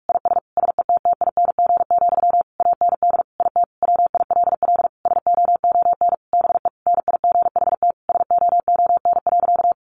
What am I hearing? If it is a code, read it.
Morse code, heard as a keyed tone, about 46 wpm, 700 Hz. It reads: IS SETTING, AND IT WILL SOON BE NIGHT SOON.
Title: Sentences from Top 1300 Words 45wpm